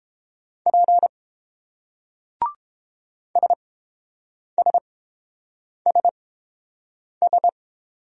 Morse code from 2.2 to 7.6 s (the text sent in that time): SSSS